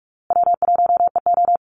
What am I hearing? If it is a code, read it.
Morse code, heard as a keyed tone, about 44 wpm, 700 Hz. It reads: W1EO
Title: USA Call Signs Repeated 3x 45wpm